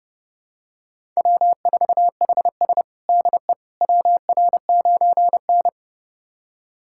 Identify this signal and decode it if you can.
Morse code, heard as a keyed tone, about 32 wpm, 700 Hz. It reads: W4HS DE WR9N
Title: USA Callsign DE Callsign - 4 Characters - 30wpm